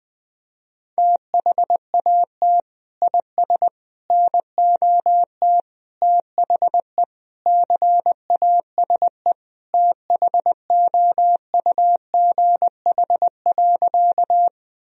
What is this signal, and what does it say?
Morse code, heard as a keyed tone, about 20 wpm, 700 Hz. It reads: THAT IS NOT THE CASE THOUGH.
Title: Sentences from Top 1000 Words - Review Part 2 - 20wpm